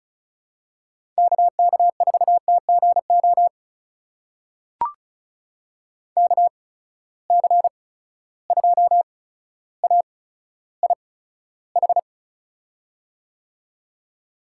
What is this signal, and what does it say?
Morse code, heard as a keyed tone, about 34 wpm, spaced as for 15 wpm, 700 Hz. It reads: KK4TGO KC2AIH
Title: USA Call Signs - 6 Characters - 8x Character Spacing - 35wpm